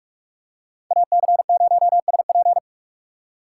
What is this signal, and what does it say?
Morse code, heard as a keyed tone, about 47 wpm, 700 Hz. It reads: AC0SP
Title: USA Call Signs - 2x Character Spacing - 45wpm